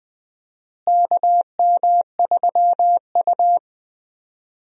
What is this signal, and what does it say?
Morse code, heard as a keyed tone, about 20 wpm, 700 Hz. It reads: KM3U
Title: USA Call Signs - 4 Characters - Repeated 2x - 20wpm